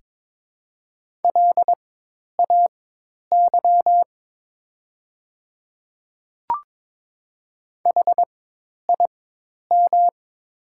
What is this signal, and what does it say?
Morse code, heard as a keyed tone, about 22 wpm, spaced as for 10 wpm, 700 Hz. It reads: LAY HIM
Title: 4x Character Spacing - Top 500 Words - 22wpm